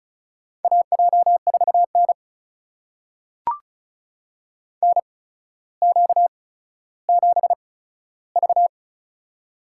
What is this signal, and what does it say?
Morse code, heard as a keyed tone, about 34 wpm, spaced as for 15 wpm, 700 Hz. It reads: AJ4N NQ7V
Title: USA Call Signs - 4 Characters - 8x Character Spacing - 35wpm